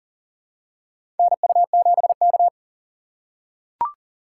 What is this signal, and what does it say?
Morse code, heard as a keyed tone, about 38 wpm, 700 Hz. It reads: NU7K
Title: USA Call Signs - 4 Characters - 2x Character Spacing - 40wpm